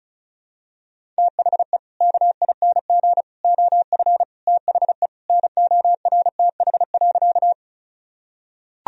Morse code, heard as a keyed tone, about 34 wpm, 700 Hz. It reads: THE KING OF THE NORTH.
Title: Sentences from Top 1000 Words - Review Part 1 - 35wpm